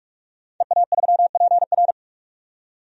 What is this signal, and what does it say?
Morse code, heard as a keyed tone, about 42 wpm, 700 Hz. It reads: EA3PR
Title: DXCC-100 Call Signs - 5 Characters - 45wpm